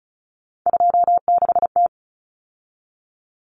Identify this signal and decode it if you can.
Morse code, heard as a keyed tone, about 34 wpm, 700 Hz. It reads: 26T
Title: Common Cut Numbers - 3-Digit 35wpm